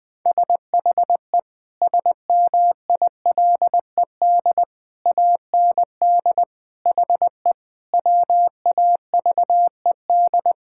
Morse code, heard as a keyed tone, about 19 wpm, 700 Hz. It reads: SHE SMILED AND HE WAVED
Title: Sentences with 2-Word Phrases - 20wpm